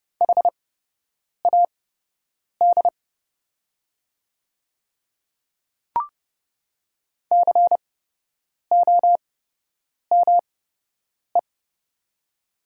Morse code, heard as a keyed tone, about 29 wpm, spaced as for 8 wpm, 690 Hz. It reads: HAD COME
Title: 8x Character Spacing - Top 300 Words - 30wpm